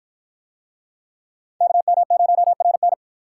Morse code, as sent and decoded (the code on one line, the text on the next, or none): -.- -- ----- .-. -.
KM0RN